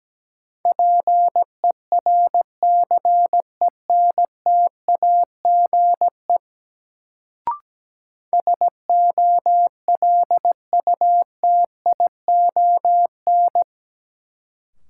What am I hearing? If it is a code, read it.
Morse code, heard as a keyed tone, about 17 wpm, 700 Hz. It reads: PERCENTAGE SOLUTION
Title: Sentences from Top 1300 Words - New Word Drill Alt - 17wpm